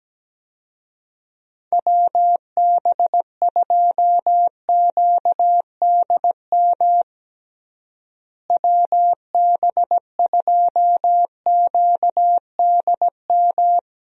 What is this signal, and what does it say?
Morse code, heard as a keyed tone, about 17 wpm, 700 Hz. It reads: WB2QDM WB2QDM
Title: USA Call Signs - 6 Characters - Repeated 3x - 17wpm